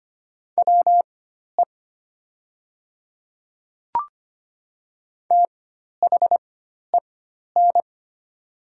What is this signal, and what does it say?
Morse code, heard as a keyed tone, about 25 wpm, spaced as for 12 wpm, 690 Hz. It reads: WE THEN
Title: 4x Character Spacing - Top 300 Words - 25wpm